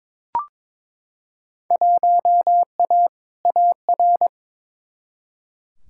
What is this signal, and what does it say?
Morse code, heard as a keyed tone, about 21 wpm, 700 Hz. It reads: 1A AR
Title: Field Day Exchanges Encore - 22wpm